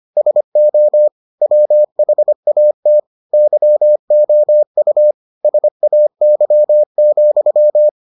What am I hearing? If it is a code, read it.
Morse code, heard as a keyed tone, about 25 wpm, 600 Hz. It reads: SO WHAT YOU SAY,